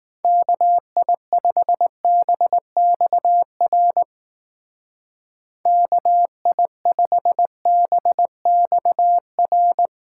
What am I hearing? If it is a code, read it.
Morse code, heard as a keyed tone, about 18 wpm, 700 Hz. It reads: KI5BXR KI5BXR
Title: Speed Racing - USA 2x3 Call Signs 15wpm